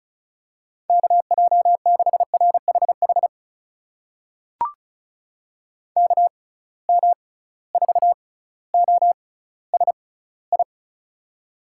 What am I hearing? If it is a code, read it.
Morse code, heard as a keyed tone, about 36 wpm, spaced as for 18 wpm, 700 Hz. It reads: KJ6RHH KM4OSI
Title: USA Call Signs - 6 Characters - 6x Character Spacing - 35wpm